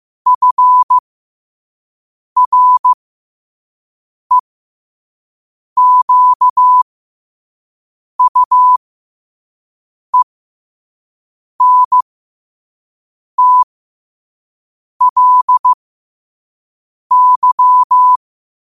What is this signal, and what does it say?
Morse code, heard as a keyed tone, about 15 wpm, spaced as for 5 wpm, 1000 Hz. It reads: FREQUENTLY
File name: qotdfiles / 5wpm